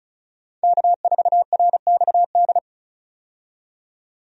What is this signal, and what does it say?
Morse code, heard as a keyed tone, about 36 wpm, 700 Hz. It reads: K4RXD
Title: USA Call Signs - 5 Characters - 35wpm